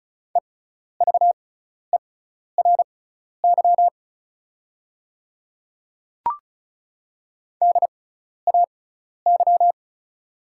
Morse code, heard as a keyed tone, about 36 wpm, spaced as for 12 wpm, 690 Hz. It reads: EVERY DAY
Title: 6x Character Spacing - Top 200 Words - 35wpm